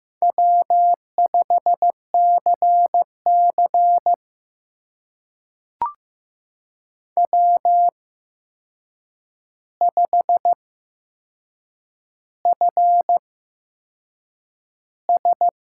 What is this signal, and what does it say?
Morse code, heard as a keyed tone, about 15 wpm, spaced as for 6 wpm, 700 Hz. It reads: W5CC W5FS
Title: USA Call Signs - 8x Character Spacing - 15wpm